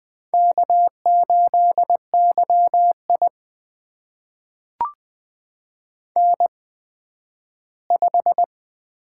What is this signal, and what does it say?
Morse code, heard as a keyed tone, about 20 wpm, spaced as for 12 wpm, 700 Hz. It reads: K8YI N5
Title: USA Call Signs - 8x Character Spacing - 20wpm